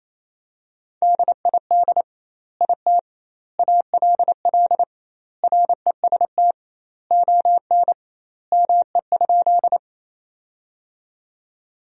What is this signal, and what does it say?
Morse code, heard as a keyed tone, about 28 wpm, 700 Hz. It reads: DID IT ALL REST ON ME?
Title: 2x Word-Spacing - Sentences from Top 500 Words - 28wpm